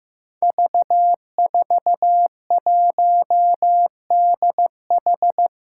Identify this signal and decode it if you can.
Morse code, heard as a keyed tone, about 15 wpm, 700 Hz. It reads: V41DH
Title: Five Challenging Letter-Number Combinations 15wpm